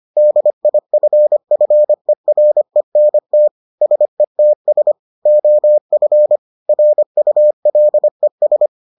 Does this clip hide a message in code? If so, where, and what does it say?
Yes: Morse code from 0.2 to 8.7 s, heard as a keyed tone, about 25 wpm, 600 Hz. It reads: DIFFERENT SETS OF RULES